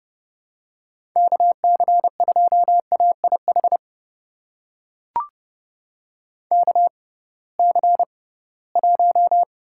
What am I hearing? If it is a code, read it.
Morse code, heard as a keyed tone, about 30 wpm, spaced as for 19 wpm, 700 Hz. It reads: KC2AIH KC1
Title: USA Call Signs - 6 Characters - 6x Character Spacing - 30wpm